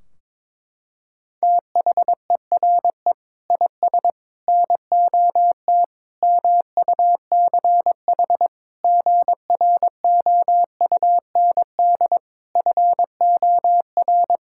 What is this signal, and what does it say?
Morse code, heard as a keyed tone, about 22 wpm, 700 Hz. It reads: THERE IS NOT MUCH GROUND FOR
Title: Sentences from Top 1600 Words 22wpm